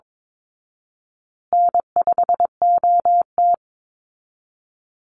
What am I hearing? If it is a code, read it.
Morse code, heard as a keyed tone, about 22 wpm, 700 Hz. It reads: N5OT
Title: Easy Call Signs Repeated 3x - 22wpm